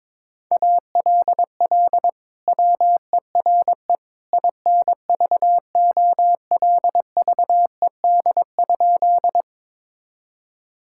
Morse code, heard as a keyed tone, about 22 wpm, 700 Hz. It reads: ALL WERE INVOLVED?
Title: Sentences from Top 1000 Words - Review Part 5 - 22wpm